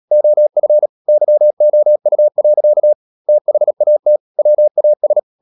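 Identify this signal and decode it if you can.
Morse code, heard as a keyed tone, about 37 wpm, 600 Hz. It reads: OF YOU. THAT WAS
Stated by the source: Morse code practice at 35–39 wpm sending English text of Ken Follett's novel Never, from CW med Gnister.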